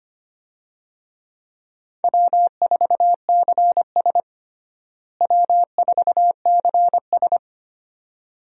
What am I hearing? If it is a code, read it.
Morse code, heard as a keyed tone, about 25 wpm, 700 Hz. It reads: W4CS W4CS
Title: USA Call Signs - 4 Characters - Repeated 2x - 25wpm